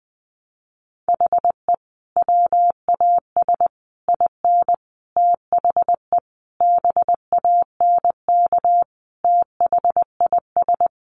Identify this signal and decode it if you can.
Morse code, heard as a keyed tone, about 20 wpm, 700 Hz. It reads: HE WAS IN THE BANK THIS
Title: Sentences from Top 900 Words 20wpm